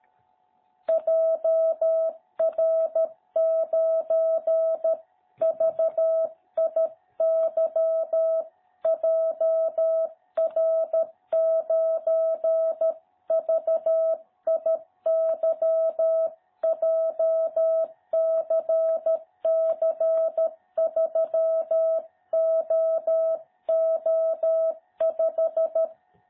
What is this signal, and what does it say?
Morse code, heard as a keyed tone, about 13 wpm, 660 Hz. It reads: JR9VIY JR9VIY JCC3OO5
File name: jh8cbh / 2021cw